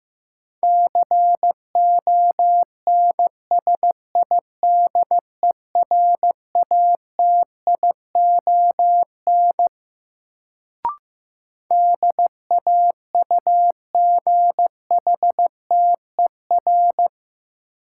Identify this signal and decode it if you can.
Morse code, heard as a keyed tone, about 15 wpm, 700 Hz. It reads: CONSIDERATION DAUGHTER
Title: Sentences from Top 1500 Words - New Word Drill - 15wpm